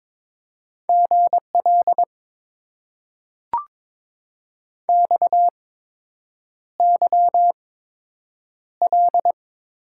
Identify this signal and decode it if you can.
Morse code, heard as a keyed tone, about 22 wpm, spaced as for 8 wpm, 700 Hz. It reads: GL XYL
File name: Top 90 QSO Elements - 8x Character Spacing - 22wpm